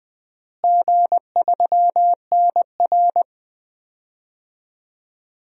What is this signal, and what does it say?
Morse code, heard as a keyed tone, about 21 wpm, 700 Hz. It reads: G3NR
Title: DXCC-100 Call Signs - 4 Characters - Repeated 2x - 20wpm